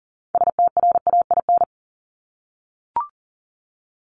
Morse code, heard as a keyed tone, about 40 wpm, 700 Hz. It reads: STRAIN
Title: Top 1751-2000 Words 40wpm